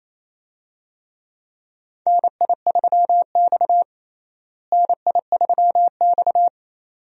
Morse code, heard as a keyed tone, about 28 wpm, 700 Hz. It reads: NI3X NI3X
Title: USA Call Signs - 4 Characters - Repeated 2x - 28wpm